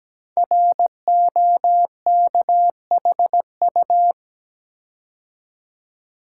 Morse code, heard as a keyed tone, about 17 wpm, 700 Hz. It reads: ROKHU
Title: Code Groups - 5 Letters - 17wpm